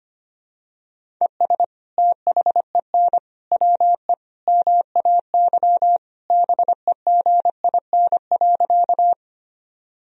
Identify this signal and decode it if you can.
Morse code, heard as a keyed tone, about 25 wpm, 700 Hz. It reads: ES THEN WE MAY BEGIN.
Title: Sentences from Top 300 Words and Abbreviations 25wpm